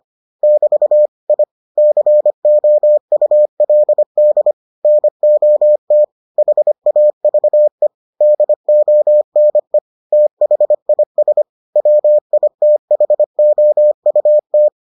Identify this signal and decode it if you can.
Morse code, heard as a keyed tone, about 25 wpm, 600 Hz. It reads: <BT> I COULD NOT HAVE DONE THIS WITHOUT